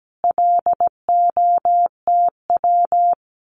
Morse code, heard as a keyed tone, about 17 wpm, 700 Hz. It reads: LOTW
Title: Common QSO Phrases Encore 17wpm